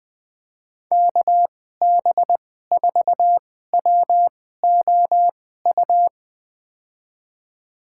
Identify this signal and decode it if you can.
Morse code, heard as a keyed tone, about 20 wpm, spaced as for 14 wpm, 700 Hz. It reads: KB4WOU
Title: USA Call Signs - 6 Characters - 2x Character Spacing - 20wpm